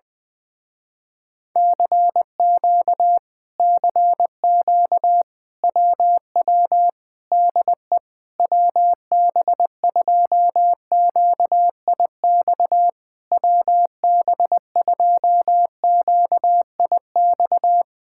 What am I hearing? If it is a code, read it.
Morse code, heard as a keyed tone, about 20 wpm, 700 Hz. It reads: CQ CQ WW DE WB2QIX WB2QIX
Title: Speed-Racing Common QSO Phrases - 20wpm